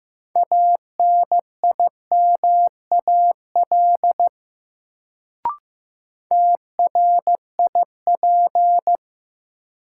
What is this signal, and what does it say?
Morse code, heard as a keyed tone, about 15 wpm, 700 Hz. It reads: ANIMAL TRIP
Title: Sentences from Top 1300 Words - New Word Drill Alt - 15wpm